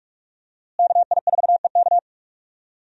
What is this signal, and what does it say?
Morse code, heard as a keyed tone, about 42 wpm, 700 Hz. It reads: KI4EK
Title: USA Call Signs - 5 Characters - Repeated 3x - 45wpm